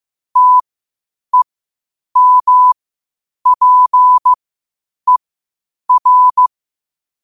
Morse code, heard as a keyed tone, about 15 wpm, spaced as for 8 wpm, 1000 Hz. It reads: TEMPER
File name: qotdfiles / 8wpm